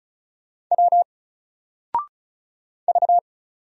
Morse code, heard as a keyed tone, about 34 wpm, 700 Hz. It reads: W V
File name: Single-Letter-Number-35wpm